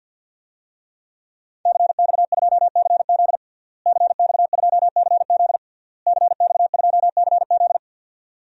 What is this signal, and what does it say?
Morse code, heard as a keyed tone, about 52 wpm, 700 Hz. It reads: CX2CZ CX2CZ CX2CZ
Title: DXCC-100 Call Signs - 5 Characters - Repeated 3x - 50wpm